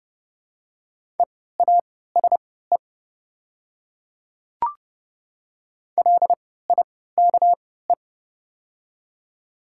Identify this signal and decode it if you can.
Morse code, heard as a keyed tone, about 30 wpm, spaced as for 17 wpm, 700 Hz. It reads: EASE LIKE
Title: 3x Character Spacing - Top 500 Words - 30wpm